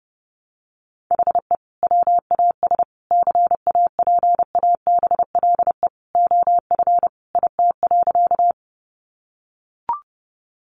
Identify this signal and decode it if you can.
Morse code, heard as a keyed tone, about 30 wpm, 700 Hz. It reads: HE WAS CAPABLE OF IT.
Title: Sentences from Top 1200 Words 30wpm